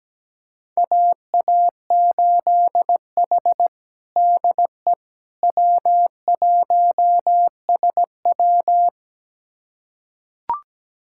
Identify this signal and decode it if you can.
Morse code, heard as a keyed tone, about 17 wpm, 700 Hz. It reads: AA8H DE W1SW
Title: USA Callsign DE Callsign - 4 Characters - 17wpm